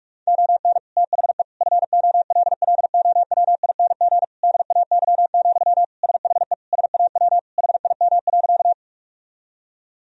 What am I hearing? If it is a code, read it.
Morse code, heard as a keyed tone, about 46 wpm, 700 Hz. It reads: ON THE FOLLOWING DAY, SHE SAW HIM.
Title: Sentences from Top 1000 Words - Review Part 4 - 45wpm